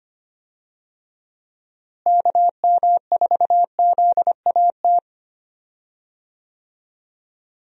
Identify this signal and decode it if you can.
Morse code, heard as a keyed tone, about 25 wpm, 700 Hz. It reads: KM4ZAT
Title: USA Call Signs - 6 Characters - Repeated 3x - 25wpm